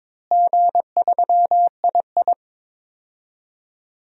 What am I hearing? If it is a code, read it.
Morse code, heard as a keyed tone, about 22 wpm, 700 Hz. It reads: G3II
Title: DXCC-100 Call Signs - 4 Characters - Repeated 2x - 22wpm